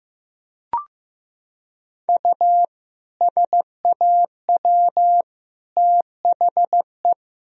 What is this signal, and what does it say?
Morse code, heard as a keyed tone, about 15 wpm, 700 Hz. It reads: U SAW THE
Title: Sentences from Top 400 Words and Abbreviations 15wpm